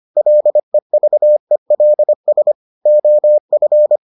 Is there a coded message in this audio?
Yes: Morse code from 0.2 to 4.0 s, heard as a keyed tone, about 25 wpm, 600 Hz. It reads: LEVELS OF